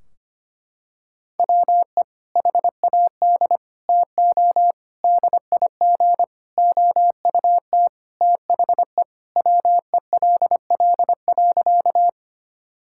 Morse code, heard as a keyed tone, about 25 wpm, 700 Hz. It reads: WE HAD TO DIG OUT THE WELL.
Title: Sentences from Top 1600 Words 25wpm